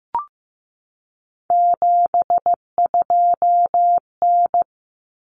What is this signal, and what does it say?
Morse code, heard as a keyed tone, about 15 wpm, 700 Hz. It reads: 72N
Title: Common Cut Numbers - 3-Digit 15wpm